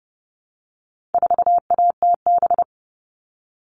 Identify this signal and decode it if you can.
Morse code, heard as a keyed tone, about 30 wpm, 700 Hz. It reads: 4ATB
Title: All Cut Numbers - 4-Digit 30wpm